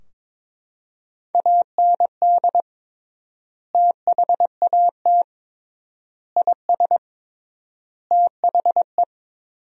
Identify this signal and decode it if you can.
Morse code, heard as a keyed tone, about 22 wpm, 700 Hz. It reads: AND THAT IS THE
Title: 3x Word-Spacing - Sentences from Top 400 Words - 22wpm